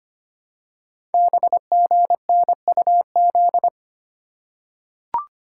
Morse code, heard as a keyed tone, about 25 wpm, 700 Hz. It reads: BGNUZ
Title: Code Groups - 5 Letters - 25wpm